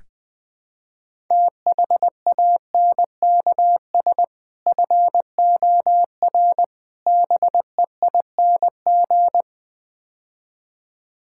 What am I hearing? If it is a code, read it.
Morse code, heard as a keyed tone, about 21 wpm, 700 Hz. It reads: THANKS FOR BEING
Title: Build A Phrase - 5-Word Phrases - 20wpm